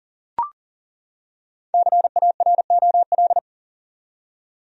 Morse code, heard as a keyed tone, about 40 wpm, 700 Hz. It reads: CAROL
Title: USA Common Names - 40wpm